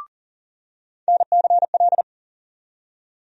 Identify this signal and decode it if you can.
Morse code, heard as a keyed tone, about 40 wpm, 700 Hz. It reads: NCL
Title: USA Call Sign Postfixes - 3 Letters 40wpm